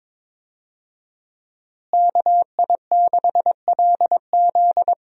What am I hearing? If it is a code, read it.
Morse code, heard as a keyed tone, about 23 wpm, 700 Hz. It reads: KI6LZ
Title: USA Call Signs - 5 Characters - Repeated 3x - 22wpm